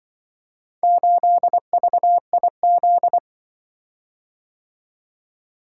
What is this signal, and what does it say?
Morse code, heard as a keyed tone, about 24 wpm, 700 Hz. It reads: 8VIZ
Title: Speed-Racing - Four Letter-Number Combinations - 20wpm